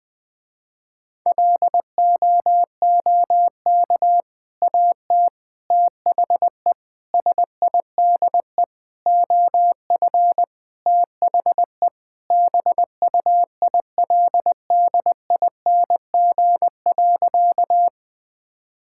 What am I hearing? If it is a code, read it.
Morse code, heard as a keyed tone, about 20 wpm, 700 Hz. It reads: LOOK AT THE SIDE OF THE BUILDING.
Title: Sentences with 3-Word Phrases - Spoken Beginning - 20wpm